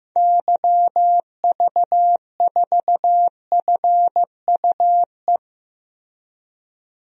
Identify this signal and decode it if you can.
Morse code, heard as a keyed tone, about 15 wpm, 700 Hz. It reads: YV4FUE
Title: DXCC-100 Call Signs - 6 Characters - Repeated 2x - 15wpm